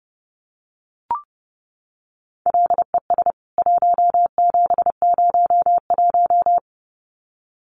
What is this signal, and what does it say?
Morse code, heard as a keyed tone, about 30 wpm, 700 Hz. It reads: LES 1701
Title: CWT Exchange 30wpm